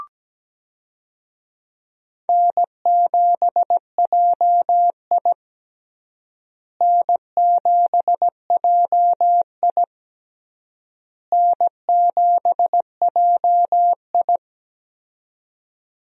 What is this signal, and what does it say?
Morse code, heard as a keyed tone, about 17 wpm, 700 Hz. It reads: N7JI N7JI N7JI
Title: USA Call Signs - 4 Characters - Repeated 3x - 17wpm